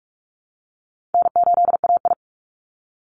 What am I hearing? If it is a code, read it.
Morse code, heard as a keyed tone, about 46 wpm, 700 Hz. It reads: N8AI
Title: Easy Call Signs - 45wpm